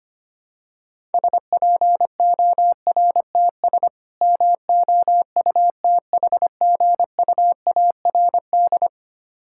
Morse code, heard as a keyed tone, about 25 wpm, 700 Hz. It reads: SPORTS MOUTHGUARD
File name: practice / 25wpm / Difficult Words - Speed-Racing with Hints - 25wpm